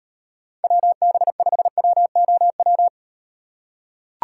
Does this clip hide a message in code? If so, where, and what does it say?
Yes: Morse code from 0.6 to 2.9 s, heard as a keyed tone, about 38 wpm, 700 Hz. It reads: WB5WOW